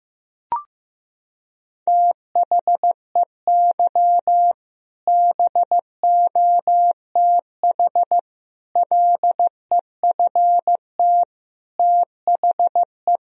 Morse code, heard as a keyed tone, about 15 wpm, 700 Hz. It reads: THEY BOTH LEFT THE
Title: Sentences from Top 1000 Words - Review Part 1 - 15wpm